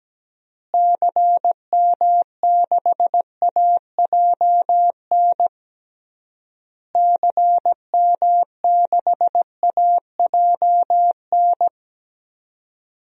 Morse code, heard as a keyed tone, about 17 wpm, 700 Hz. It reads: CM6AJN CM6AJN
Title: DXCC-100 Call Signs - 6 Characters - Repeated 3x - 17wpm